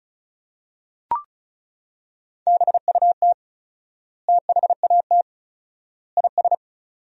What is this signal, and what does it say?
Morse code, heard as a keyed tone, about 36 wpm, 700 Hz. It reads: BUT THAT IS
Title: 4x Word-Spacing - Sentences from Top 500 Words - 35wpm